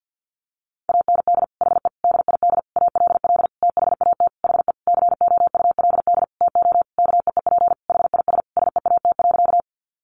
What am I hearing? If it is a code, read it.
Morse code, heard as a keyed tone, about 50 wpm, 700 Hz. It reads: AND HE DID ALL THAT HE COULD TO KEEP HIS SEAT.
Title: Sentences from Top 1500 Words 50wpm